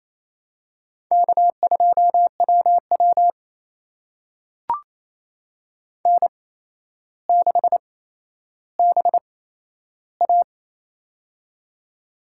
Morse code, heard as a keyed tone, about 28 wpm, spaced as for 12 wpm, 700 Hz. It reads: K2WW N6BA
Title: USA Call Signs - 4 Characters - 8x Character Spacing - 28wpm